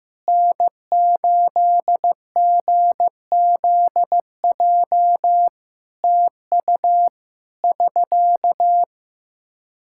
Morse code, heard as a keyed tone, about 16 wpm, 700 Hz. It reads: N8GZJ TU <SK>
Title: Practice QSO - Exchange 4 - Ending - 15wpm